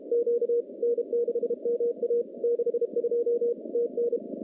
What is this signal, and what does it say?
Morse code, heard as a keyed tone, about 31 wpm, 470 Hz. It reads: Q N6MA 62 TN
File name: contests / CW-SS-k5tr- -1400